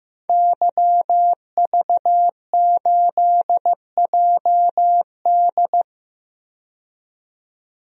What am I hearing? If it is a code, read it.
Morse code, heard as a keyed tone, about 15 wpm, 700 Hz. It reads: YV8JD
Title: DXCC-100 Call Signs - 5 Characters - 15wpm